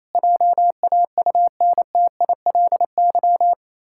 Morse code, heard as a keyed tone, about 28 wpm, 700 Hz. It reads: JAUNTILY
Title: Pangram Challenge v2 - 28wpm